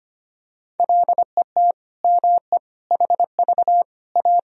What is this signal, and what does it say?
Morse code, heard as a keyed tone, about 25 wpm, 700 Hz. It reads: LET ME HV A
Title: Sentences from Top 300 Words and Abbreviations 25wpm